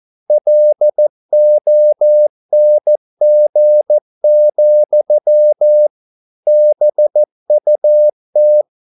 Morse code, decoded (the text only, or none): LONG, BUT